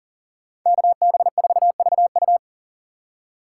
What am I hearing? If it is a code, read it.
Morse code, heard as a keyed tone, about 40 wpm, 700 Hz. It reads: KB4VU
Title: K1USN SST QSO Practice - 40wpm